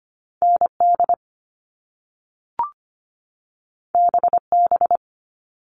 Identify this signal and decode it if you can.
Morse code, heard as a keyed tone, about 25 wpm, 700 Hz. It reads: ND BB
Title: All Cut Numbers - 2-Digit 25wpm